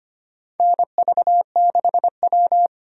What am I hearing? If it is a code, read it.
Morse code, heard as a keyed tone, about 25 wpm, 700 Hz. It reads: NV6W
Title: USA Call Signs - 4 Characters - 25wpm